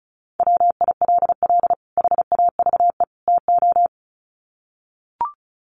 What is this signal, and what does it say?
Morse code, heard as a keyed tone, about 34 wpm, 700 Hz. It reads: WILL HAVE TO
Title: Sets of 3 Words 35wpm